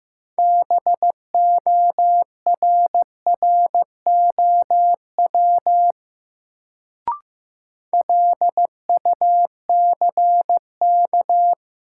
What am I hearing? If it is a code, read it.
Morse code, heard as a keyed tone, about 15 wpm, 700 Hz. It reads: BORROW LUCK
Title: Sentences from Top 1500 Words - New Word Drill Alt - 15wpm